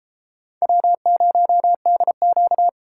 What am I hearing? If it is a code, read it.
Morse code, heard as a keyed tone, about 32 wpm, 700 Hz. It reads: W0DQ